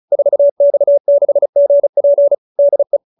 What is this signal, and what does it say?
Morse code, heard as a keyed tone, about 35 wpm, 590 Hz. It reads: 4X6GP DE
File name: QSO310 35